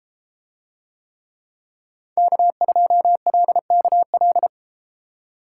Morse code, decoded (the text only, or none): K2LKL